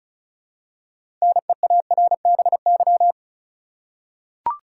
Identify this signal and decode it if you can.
Morse code, heard as a keyed tone, about 36 wpm, 700 Hz. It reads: NEARBY
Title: Six Letter Words - 35wpm